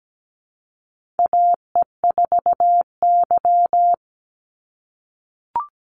Morse code, heard as a keyed tone, about 17 wpm, 700 Hz. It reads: AE4Y
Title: NA SOTA Chaser Callsigns 17wpm